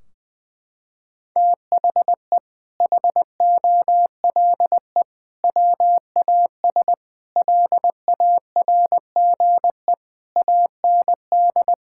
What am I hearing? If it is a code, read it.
Morse code, heard as a keyed tone, about 20 wpm, 700 Hz. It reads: THE HOLE WAS LARGE AND
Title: Sentences from Top 1600 Words 20wpm